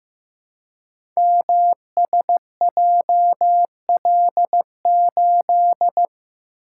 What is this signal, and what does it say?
Morse code, heard as a keyed tone, about 15 wpm, 700 Hz. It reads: MSJL8
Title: Code Groups - 5 Letter:Numbers - 15wpm